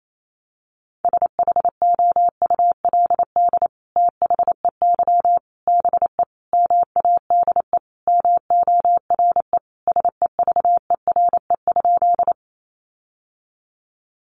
Morse code, heard as a keyed tone, about 28 wpm, 700 Hz. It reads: SHOULD THEY BE MADE MORE SEVERE?
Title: Sentences from Top 1400 Words 28wpm